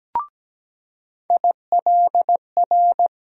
Morse code, heard as a keyed tone, about 17 wpm, 700 Hz. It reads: ILR
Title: USA Call Sign Postfixes - 3 Letters 17wpm